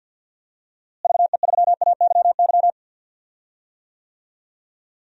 Morse code, heard as a keyed tone, about 52 wpm, 700 Hz. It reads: VE3AYY
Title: DXCC-100 Call Signs - 6 Characters - Repeated 3x - 50wpm